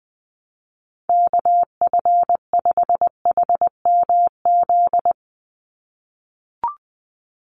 Morse code, heard as a keyed tone, about 20 wpm, 700 Hz. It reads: KF5HMZ
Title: USA Call Signs 20wpm